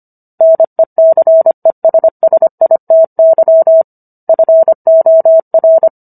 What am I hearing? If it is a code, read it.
Morse code, heard as a keyed tone, about 25 wpm, 650 Hz. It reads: NECESSITY FOR